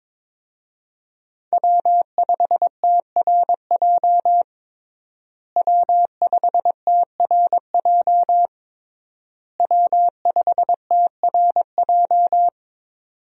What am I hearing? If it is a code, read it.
Morse code, heard as a keyed tone, about 22 wpm, 700 Hz. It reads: W5TRJ W5TRJ W5TRJ
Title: USA Call Signs - 5 Characters - Repeated 3x - 22wpm